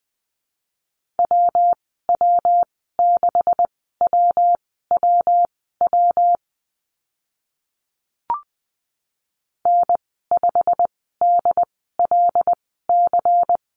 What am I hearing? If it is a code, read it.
Morse code, heard as a keyed tone, about 20 wpm, spaced as for 14 wpm, 700 Hz. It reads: WW6WWW N5DLC
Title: 140 Lesson 23g - Letter f - Call Signs 20wpm